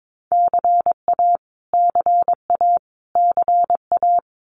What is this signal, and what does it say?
Morse code, heard as a keyed tone, about 22 wpm, 700 Hz. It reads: CA CA CA
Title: US States Repeated 3x 22wpm